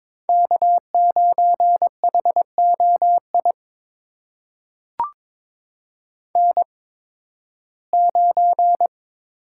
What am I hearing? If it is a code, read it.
Morse code, heard as a keyed tone, about 22 wpm, spaced as for 14 wpm, 700 Hz. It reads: K9HOI N9
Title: USA Call Signs - 8x Character Spacing - 22wpm